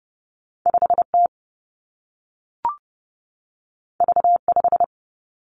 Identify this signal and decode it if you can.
Morse code, heard as a keyed tone, about 30 wpm, 700 Hz. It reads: 5 T V 5
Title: Two Challenging Letter-Number Combinations 30wpm